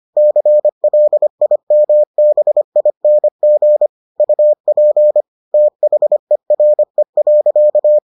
Morse code, heard as a keyed tone, about 25 wpm, 600 Hz. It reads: CLIMBING UP THERE.